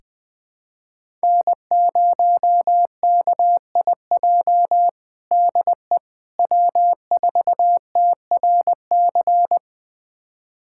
Morse code, heard as a keyed tone, about 20 wpm, 700 Hz. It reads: N0KIJ DE W4TRC
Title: USA Callsign DE Callsign - 5 Characters - 20wpm